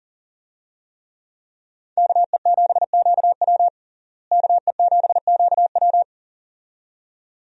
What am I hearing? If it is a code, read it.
Morse code, heard as a keyed tone, about 40 wpm, 700 Hz. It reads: KE7QW KE7QW
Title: USA Call Signs - 5 Characters - Repeated 2x - 40wpm